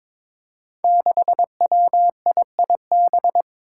Morse code, heard as a keyed tone, about 22 wpm, 700 Hz. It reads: 6WIIB
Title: Five Challenging Letter-Number Combinations 22wpm